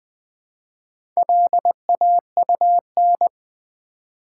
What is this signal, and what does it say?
Morse code, heard as a keyed tone, about 20 wpm, 690 Hz. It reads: LAUN
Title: Build A Word - 2 Letters At A Time - 6-Letter Words - 20wpm